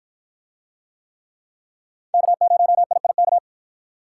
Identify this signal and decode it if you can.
Morse code, heard as a keyed tone, about 47 wpm, 700 Hz. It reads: K0IIK